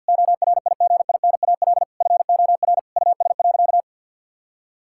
Morse code, heard as a keyed tone, about 50 wpm, 700 Hz. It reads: ORIGINAL FOR US.
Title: Sentences from Top 1000 Words - Review Part 6 - 50wpm